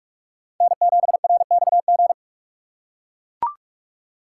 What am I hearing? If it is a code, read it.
Morse code, heard as a keyed tone, about 44 wpm, 700 Hz. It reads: N7RXG
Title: USA Call Signs - 5 Characters - 6x Character Spacing - 45wpm